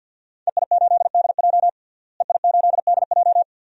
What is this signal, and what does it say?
Morse code, heard as a keyed tone, about 50 wpm, 700 Hz. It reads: EI8DJ EI8DJ
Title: DXCC-100 Call Signs - 5 Characters - Repeated 2x - 50wpm